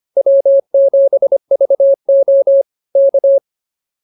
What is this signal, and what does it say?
Morse code, heard as a keyed tone, about 24 wpm, 550 Hz. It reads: W7VO K
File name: QSO105 25